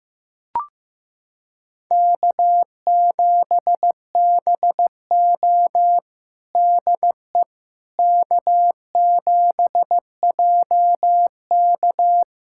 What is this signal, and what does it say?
Morse code, heard as a keyed tone, about 15 wpm, 700 Hz. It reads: K7BO DE K7JK
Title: USA Callsign DE Callsign - 4 Characters - 15wpm